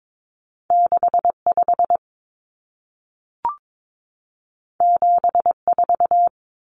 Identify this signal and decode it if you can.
Morse code, heard as a keyed tone, about 22 wpm, 700 Hz. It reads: 65 74
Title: Numbers - 2-Digit 22wpm